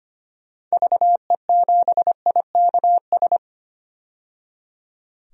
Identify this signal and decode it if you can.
Morse code, heard as a keyed tone, about 25 wpm, 700 Hz. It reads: VE7IKS
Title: DXCC-100 Call Signs - 6 Characters - 25wpm